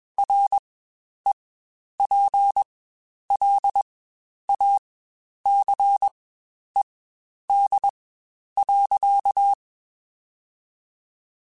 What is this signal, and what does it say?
Morse code, heard as a keyed tone, about 21 wpm, spaced as for 10 wpm, 800 Hz. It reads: REPLACED.
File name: Spaced / Chuckles Spaced-mp3s / Squeaky sp @20